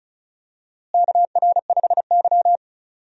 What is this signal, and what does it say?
Morse code, heard as a keyed tone, about 34 wpm, 700 Hz. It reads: KR5Y
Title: USA Call Signs - 4 Characters - Repeated 2x - 35wpm